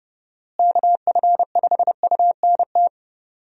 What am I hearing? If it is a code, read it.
Morse code, heard as a keyed tone, about 30 wpm, 700 Hz. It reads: KF5UNT
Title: USA Call Signs - 6 Characters - Repeated 3x - 30wpm